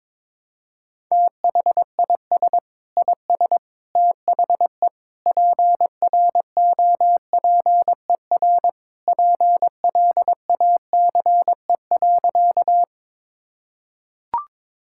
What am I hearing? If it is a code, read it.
Morse code, heard as a keyed tone, about 22 wpm, 700 Hz. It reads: THIS IS THE PROPER PLACE.
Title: Sentences from Top 1000 Words - Review Part 5 - 22wpm